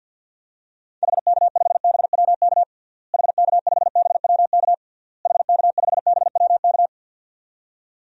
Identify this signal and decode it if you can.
Morse code, heard as a keyed tone, about 51 wpm, 700 Hz. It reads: HK5BWK HK5BWK HK5BWK
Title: DXCC-100 Call Signs - 6 Characters - Repeated 3x - 50wpm